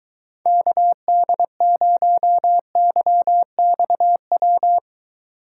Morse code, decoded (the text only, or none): KD0YXW